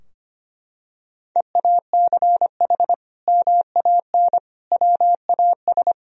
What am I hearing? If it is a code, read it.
Morse code, heard as a keyed tone, about 25 wpm, 700 Hz. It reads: EACH MAN WAS
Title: Sentences from Top 1000 Words - Review Part 2 - 25wpm